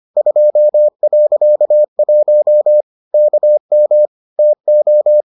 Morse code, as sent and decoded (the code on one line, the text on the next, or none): ..--- .-.-.- .---- / -.- -- / - ---
2.1 KM TO